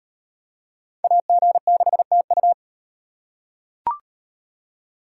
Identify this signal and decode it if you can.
Morse code, heard as a keyed tone, about 38 wpm, 700 Hz. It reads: AG6TU